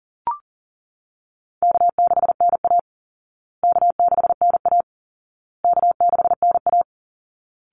Morse code, heard as a keyed tone, about 37 wpm, 700 Hz. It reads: K6NA K6NA K6NA
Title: Easy Call Signs Repeated 3x - 40wpm